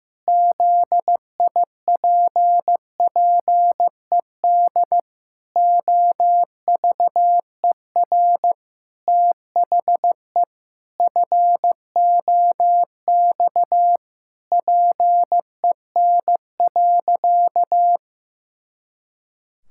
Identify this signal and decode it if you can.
Morse code, heard as a keyed tone, about 15 wpm, 700 Hz. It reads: ZIPPED OVER THE FOX PEN.
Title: Pangram Challenge v2 - 15wpm